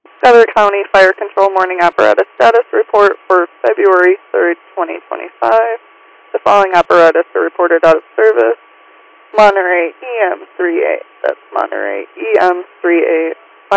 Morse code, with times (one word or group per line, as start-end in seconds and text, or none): none